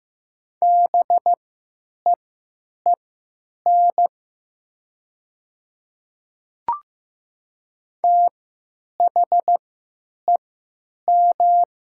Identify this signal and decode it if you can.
Morse code, heard as a keyed tone, about 15 wpm, spaced as for 9 wpm, 700 Hz. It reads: BEEN THEM
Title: 3x Character Spacing - Top 100 Words - 15wpm